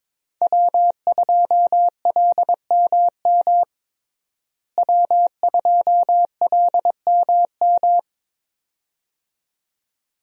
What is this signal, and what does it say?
Morse code, heard as a keyed tone, about 22 wpm, 700 Hz. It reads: W2LMM W2LMM
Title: USA Call Signs - 5 Characters - Repeated 2x - 22wpm